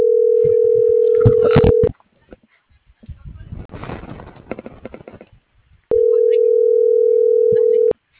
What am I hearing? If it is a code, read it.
Ringback tone (North America), 440 + 480 Hz.